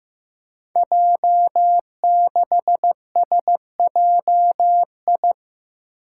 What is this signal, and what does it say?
Morse code, heard as a keyed tone, about 15 wpm, 700 Hz. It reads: J6SJI
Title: Five Challenging Letter-Number Combinations 15wpm